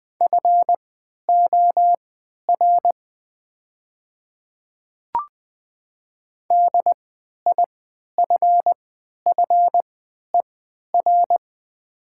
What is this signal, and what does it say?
Morse code, heard as a keyed tone, about 20 wpm, spaced as for 11 wpm, 700 Hz. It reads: FOR DIFFER
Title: 3x Character Spacing - Top 200 Words - 20wpm